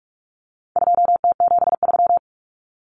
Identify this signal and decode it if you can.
Morse code, heard as a keyed tone, about 44 wpm, 700 Hz. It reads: 2T73
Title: Common Cut Numbers - 4-Digit 45wpm